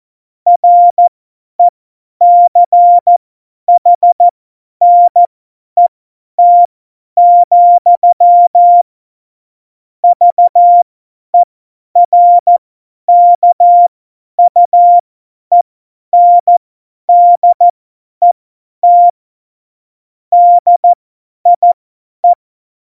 Morse code, heard as a keyed tone, about 14 wpm, spaced as for 10 wpm, 700 Hz. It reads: RECHNET, VERKUENDET DIE
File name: Morsekurs / Witze / Witz 23. April 2019, 50 F70